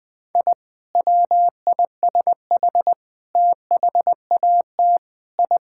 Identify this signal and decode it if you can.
Morse code, heard as a keyed tone, about 20 wpm, 700 Hz. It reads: I WISH THAT I
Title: Sentences with 2-Word Phrases - 20wpm